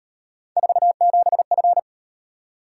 Morse code, heard as a keyed tone, about 37 wpm, 700 Hz. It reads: 47F